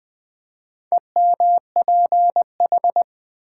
Morse code, heard as a keyed tone, about 20 wpm, 700 Hz. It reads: EMPH
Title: Build A Word - 2 Letters At A Time - 8-Letter Words - 20wpm